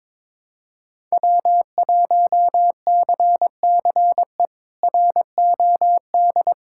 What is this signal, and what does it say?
Morse code, heard as a keyed tone, about 22 wpm, 700 Hz. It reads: W1CCE ROD
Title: K1USN SST QSO Practice - 22wpm